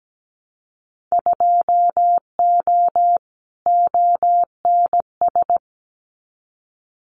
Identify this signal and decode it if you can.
Morse code, heard as a keyed tone, about 17 wpm, 700 Hz. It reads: 2O ONS
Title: Winter Field Day Exchanges Encore 17wpm